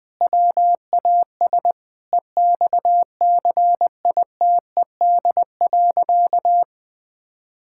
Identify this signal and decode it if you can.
Morse code, heard as a keyed tone, about 20 wpm, 700 Hz. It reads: WAS EXCITED.
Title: Sentences with 5-Word Phrases - 20wpm